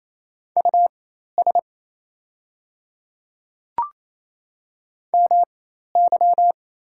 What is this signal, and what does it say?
Morse code, heard as a keyed tone, about 27 wpm, spaced as for 13 wpm, 700 Hz. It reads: US MY
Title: 4x Character Spacing - Top 200 Words - 28wpm